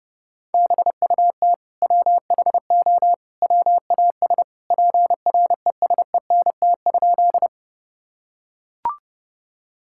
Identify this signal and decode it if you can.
Morse code, heard as a keyed tone, about 32 wpm, 700 Hz. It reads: BUT WHO WAS PRESENT?
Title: Sentences from Top 1000 Words - Review Part 5 - 30wpm